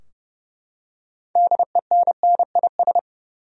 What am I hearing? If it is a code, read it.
Morse code, heard as a keyed tone, about 32 wpm, 700 Hz. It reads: DENNIS
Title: USA Common Names - 30wpm